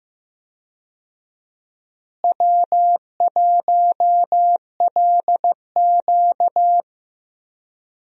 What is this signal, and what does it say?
Morse code, heard as a keyed tone, about 15 wpm, 700 Hz. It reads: W1LQ
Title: USA Call Signs - 4 Characters - Repeated 3x - 15wpm